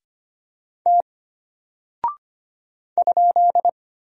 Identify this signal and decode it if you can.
Morse code, heard as a keyed tone, about 25 wpm, 700 Hz. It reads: T ?
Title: Single Character (Letter:Number:Punctuation:Prosign) Alt 25wpm